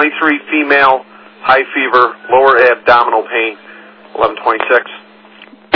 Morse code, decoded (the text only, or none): none